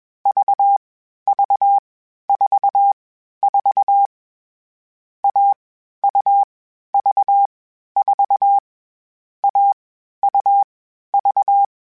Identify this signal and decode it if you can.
Morse code, heard as a keyed tone, about 21 wpm, 800 Hz. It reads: V V 4 4 A U V 4 A U V